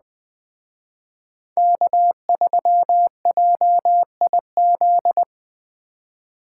Morse code, heard as a keyed tone, about 20 wpm, 700 Hz. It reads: K3JIZ
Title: Speed Racing - USA Call Signs 17wpm